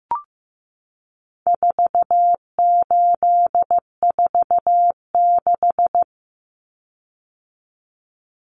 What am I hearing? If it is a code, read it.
Morse code, heard as a keyed tone, about 15 wpm, 700 Hz. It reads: 4846
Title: Common Cut Numbers - 4-Digit 15wpm